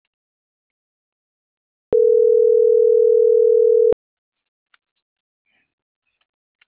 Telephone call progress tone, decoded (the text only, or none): ringback tone